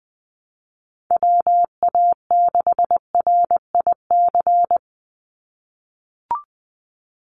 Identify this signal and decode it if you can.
Morse code, heard as a keyed tone, about 20 wpm, 700 Hz. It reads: WA6RIC
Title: NA SOTA Chaser Callsigns 20wpm